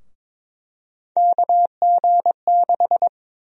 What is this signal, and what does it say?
Morse code, heard as a keyed tone, about 23 wpm, 700 Hz. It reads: KG6
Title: USA Call Sign Prefixes - Repeated 2x - 22wpm